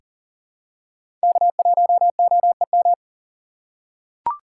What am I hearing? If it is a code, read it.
Morse code, heard as a keyed tone, about 38 wpm, 700 Hz. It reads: K1OEM
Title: USA Call Signs - 5 Characters - 6x Character Spacing - 40wpm